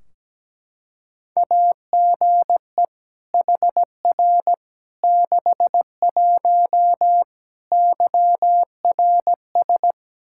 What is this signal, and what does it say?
Morse code, heard as a keyed tone, about 17 wpm, 700 Hz. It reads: AGE HR 61 YRS
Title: Practice QSO - Exchange 3 - More Info - 17wpm